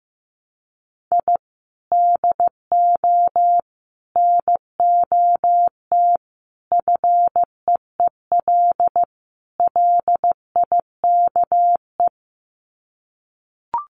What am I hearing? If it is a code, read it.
Morse code, heard as a keyed tone, about 15 wpm, 700 Hz. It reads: I DO NOT FEEL LIKE
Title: Sets of 5 Words Encore 15wpm